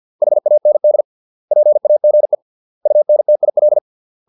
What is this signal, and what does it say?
Morse code, heard as a keyed tone, about 50 wpm, 600 Hz. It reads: HAND PAGE UNTIL